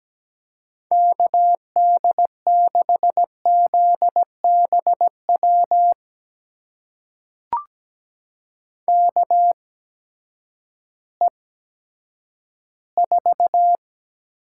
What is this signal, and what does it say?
Morse code, heard as a keyed tone, about 17 wpm, spaced as for 10 wpm, 700 Hz. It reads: KD6ZBW KE4
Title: USA Call Signs - 6 Characters - 8x Character Spacing - 17wpm